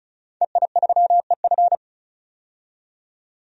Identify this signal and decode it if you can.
Morse code, heard as a keyed tone, about 36 wpm, 700 Hz. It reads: EI3EF
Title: DXCC-100 Call Signs - 5 Characters - 35wpm